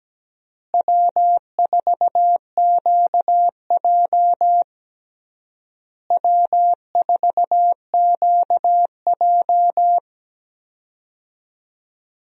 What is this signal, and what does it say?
Morse code, heard as a keyed tone, about 17 wpm, 700 Hz. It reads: W4QJ W4QJ
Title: USA Call Signs - 4 Characters - Repeated 3x - 17wpm